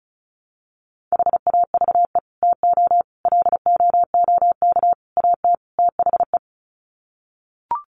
Morse code, heard as a keyed tone, about 34 wpm, 700 Hz. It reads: HAVE TO LOOK AT THE
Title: Sets of 5 Words Encore 35wpm